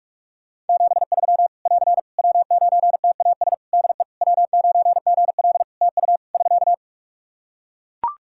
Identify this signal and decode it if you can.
Morse code, heard as a keyed tone, about 41 wpm, 700 Hz. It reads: 73 <AR> W9TAS DE W9GL TU <SK>
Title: Practice QSO - Exchange 4 - Ending - 45wpm